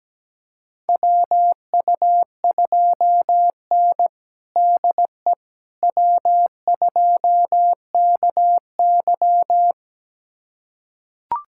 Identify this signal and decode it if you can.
Morse code, heard as a keyed tone, about 17 wpm, 700 Hz. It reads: WU2N DE W2KY
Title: USA Callsign DE Callsign - 4 Characters - 17wpm